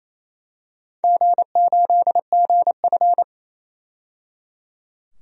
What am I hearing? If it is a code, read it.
Morse code, heard as a keyed tone, about 28 wpm, 700 Hz. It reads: G8GF
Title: DXCC-100 Call Signs - 4 Characters - 28wpm